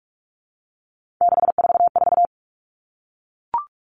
Morse code, heard as a keyed tone, about 44 wpm, 700 Hz. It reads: 644
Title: Numbers - 3-Digit 45wpm